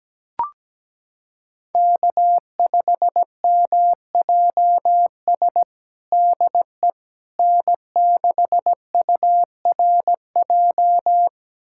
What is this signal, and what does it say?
Morse code, heard as a keyed tone, about 17 wpm, 700 Hz. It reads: K5MJS DE N6URJ
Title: USA Callsign DE Callsign - 5 Characters - 17wpm